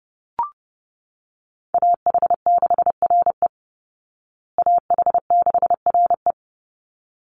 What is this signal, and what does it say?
Morse code, heard as a keyed tone, about 30 wpm, 700 Hz. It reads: AH6RE AH6RE
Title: Call Signs Repeated 3x - 30wpm